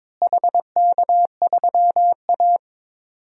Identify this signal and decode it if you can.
Morse code, heard as a keyed tone, about 22 wpm, 700 Hz. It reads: HK3A
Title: DXCC-100 Call Signs - 4 Characters - Repeated 3x - 22wpm